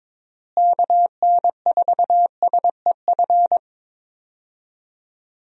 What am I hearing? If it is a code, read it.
Morse code, heard as a keyed tone, about 21 wpm, 700 Hz. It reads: KN4SEF
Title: USA Call Signs - 6 Characters - Repeated 2x - 22wpm